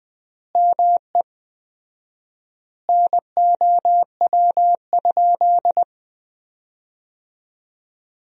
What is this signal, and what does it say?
Morse code, heard as a keyed tone, about 20 wpm, 700 Hz. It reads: ME NOW?
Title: 4x Word-Spacing - Sentences from Top 400 Words - 20wpm